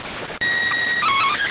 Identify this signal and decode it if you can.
SSTV (slow-scan television), PD 290.